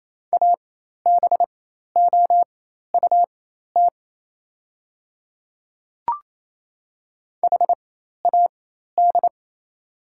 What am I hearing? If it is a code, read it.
Morse code, heard as a keyed tone, about 27 wpm, spaced as for 13 wpm, 700 Hz. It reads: ABOUT HAD
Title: 4x Character Spacing - Top 300 Words - 28wpm